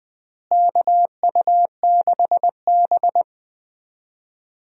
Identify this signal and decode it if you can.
Morse code, heard as a keyed tone, about 19 wpm, 700 Hz. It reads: KU6B
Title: USA Call Signs - 4 Characters - Repeated 3x - 20wpm